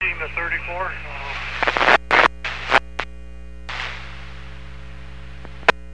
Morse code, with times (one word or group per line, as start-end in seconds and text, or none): none